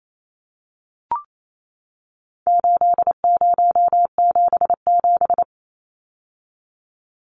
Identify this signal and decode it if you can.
Morse code, heard as a keyed tone, about 28 wpm, 700 Hz. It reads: 8077
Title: Numbers - 4-Digit 28wpm